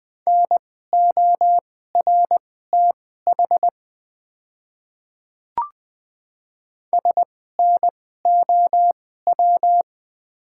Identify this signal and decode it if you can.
Morse code, heard as a keyed tone, about 20 wpm, spaced as for 14 wpm, 700 Hz. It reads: NORTH SNOW
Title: 2x Character Spacing - Top 500 Words - 20wpm